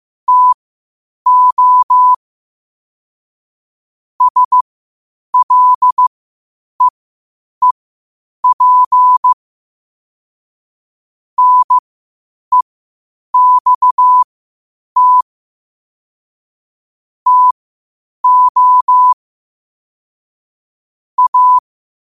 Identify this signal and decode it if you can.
Morse code, heard as a keyed tone, about 15 wpm, spaced as for 8 wpm, 1000 Hz. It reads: TO SLEEP NEXT TO A